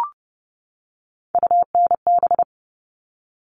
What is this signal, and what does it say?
Morse code, heard as a keyed tone, about 30 wpm, 700 Hz. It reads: UNB
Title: All Cut Numbers - 3-Digit 30wpm